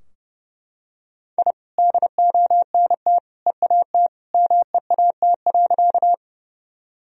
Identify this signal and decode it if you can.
Morse code, heard as a keyed tone, about 30 wpm, 700 Hz. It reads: I DONT EAT MEAT.
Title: Sentences from Top 1000 Words - Review Part 3 - 30wpm